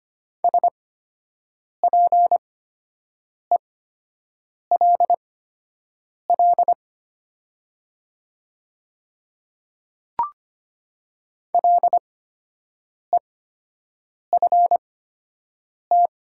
Morse code, heard as a keyed tone, about 25 wpm, spaced as for 7 wpm, 700 Hz. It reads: SPELL LEFT
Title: 8x Character Spacing - Top 300 Words - 25wpm